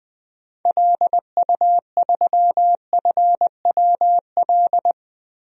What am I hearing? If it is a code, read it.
Morse code, heard as a keyed tone, about 20 wpm, 700 Hz. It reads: LU3FWL
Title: DXCC-100 Call Signs - 6 Characters - 20wpm